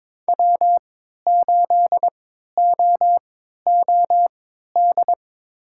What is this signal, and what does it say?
Morse code, heard as a keyed tone, about 23 wpm, spaced as for 13 wpm, 700 Hz. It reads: W8OOD
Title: USA Call Signs - 5 Characters - 3x Character Spacing - 22wpm